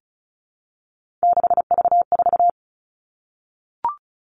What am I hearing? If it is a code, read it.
Morse code, heard as a keyed tone, about 34 wpm, 700 Hz. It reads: 6V4
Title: All Cut Numbers - 3-Digit 35wpm